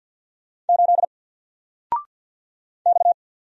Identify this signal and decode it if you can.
Morse code, heard as a keyed tone, about 50 wpm, 700 Hz. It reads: 8 X
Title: Single Character (Letter:Number:Punctuation:Prosign) 50wpm